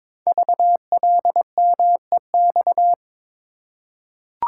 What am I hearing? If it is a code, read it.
Morse code, heard as a keyed tone, about 22 wpm, 700 Hz. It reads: VLMEX
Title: Code Groups - 5 Letters - 22wpm